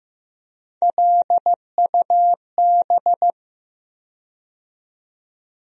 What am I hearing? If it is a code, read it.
Morse code, heard as a keyed tone, about 15 wpm, 700 Hz. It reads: LUB
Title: USA Call Sign Postfixes - 3 Letters 15wpm